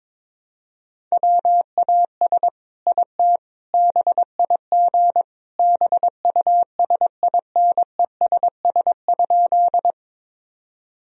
Morse code, heard as a keyed tone, about 22 wpm, 700 Hz. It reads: WAS IT BIG BUSINESS?
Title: Sentences from Top 1000 Words - Review Part 1 - 22wpm